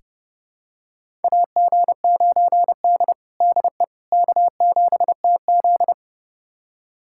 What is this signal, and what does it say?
Morse code, heard as a keyed tone, about 31 wpm, 700 Hz. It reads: AG9D DE K7TZ
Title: USA Callsign DE Callsign - 4 Characters - 30wpm